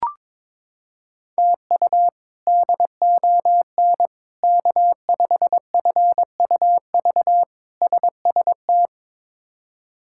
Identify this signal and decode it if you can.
Morse code, heard as a keyed tone, about 22 wpm, 700 Hz. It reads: TU DON K5FUV SST
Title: K1USN SST QSO Practice - 22wpm